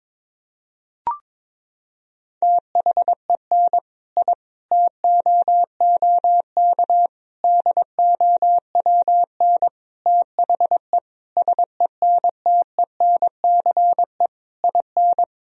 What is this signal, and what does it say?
Morse code, heard as a keyed tone, about 22 wpm, 700 Hz. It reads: THEN I TOOK DOWN THE SENTENCE IN
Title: Sentences from Top 1000 Words - Review Part 1 - 22wpm